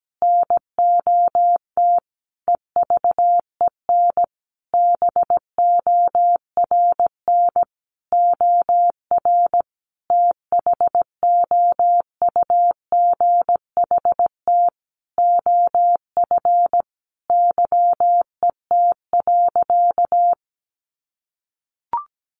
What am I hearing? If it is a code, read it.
Morse code, heard as a keyed tone, about 17 wpm, 700 Hz. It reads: NOT EVEN BORN OR THOUGHT OF YET.
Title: A Sound Of Thunder 17wpm